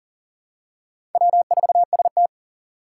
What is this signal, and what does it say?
Morse code, heard as a keyed tone, about 40 wpm, 700 Hz. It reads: W4ST
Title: Four Challenging Letter-Number Combinations 40wpm